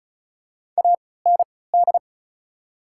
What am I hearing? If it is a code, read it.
Morse code, heard as a keyed tone, about 36 wpm, spaced as for 20 wpm, 700 Hz. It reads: AND
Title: 3x Character Spacing - Top 100 Words - 35wpm